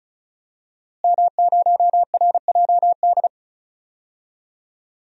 Morse code, heard as a keyed tone, about 34 wpm, 700 Hz. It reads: M0RJD
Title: DXCC-100 Call Signs - 5 Characters - Repeated 3x - 35wpm